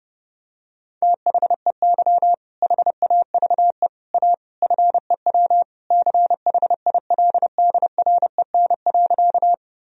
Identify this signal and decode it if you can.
Morse code, heard as a keyed tone, about 30 wpm, 700 Hz. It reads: THEY HAVE A FEW CHILDREN.
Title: Sentences from Top 1000 Words - Review Part 1 - 30wpm